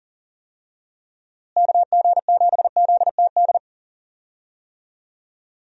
Morse code, heard as a keyed tone, about 40 wpm, 700 Hz. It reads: KG7ZTD
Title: USA Call Signs - 6 Characters - Repeated 3x - 40wpm